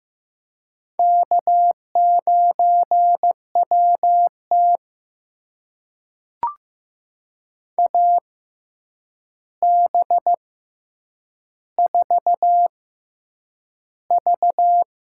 Morse code, heard as a keyed tone, about 15 wpm, spaced as for 8 wpm, 700 Hz. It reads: K9WT AB4V
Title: USA Call Signs - 4 Characters - 6x Character Spacing - 15wpm